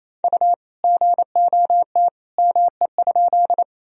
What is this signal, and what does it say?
Morse code, heard as a keyed tone, about 28 wpm, 700 Hz. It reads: U GOT ME?
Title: Sentences from Top 300 Words and Abbreviations 28wpm